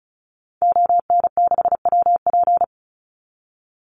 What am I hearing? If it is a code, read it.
Morse code, heard as a keyed tone, about 34 wpm, 700 Hz. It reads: ON6WP
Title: Call Signs - 35wpm